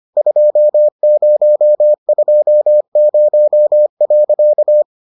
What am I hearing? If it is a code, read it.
Morse code, heard as a keyed tone, about 25 wpm, 600 Hz. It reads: 2020.